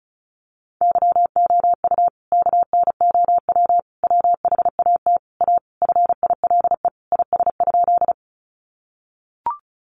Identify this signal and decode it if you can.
Morse code, heard as a keyed tone, about 34 wpm, 700 Hz. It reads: YOU KNOW WHAT A FILE IS?
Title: Sentences from Top 1200 Words 35wpm